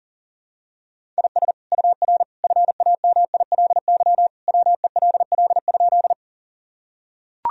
Morse code, heard as a keyed tone, about 40 wpm, 700 Hz. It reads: IS UR FAMILY WELL?
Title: Sentences from Top 400 Words and Abbreviations 40wpm